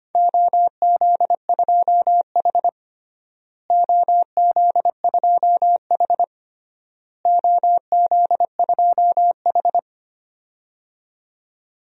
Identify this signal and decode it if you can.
Morse code, heard as a keyed tone, about 25 wpm, 700 Hz. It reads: OZ2H OZ2H OZ2H
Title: DXCC-100 Call Signs - 4 Characters - Repeated 3x - 25wpm